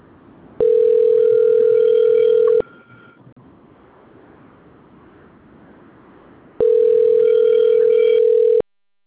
Ringback tone (North America), 440 + 480 Hz.